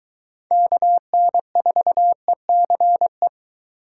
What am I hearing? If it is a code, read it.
Morse code, heard as a keyed tone, about 22 wpm, 700 Hz. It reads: KN4ECE